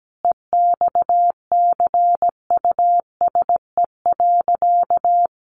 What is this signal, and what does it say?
Morse code, heard as a keyed tone, about 17 wpm, 700 Hz. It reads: EXCUSE.
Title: Sentences from Top 800 Words 17wpm